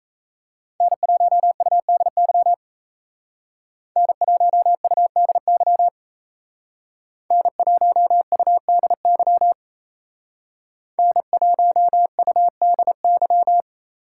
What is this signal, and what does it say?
Morse code, heard as a keyed tone, about 32 wpm, 700 Hz. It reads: N1UDY N1UDY N1UDY N1UDY
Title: Speed Racing - USA 1x3 Call Signs 28wpm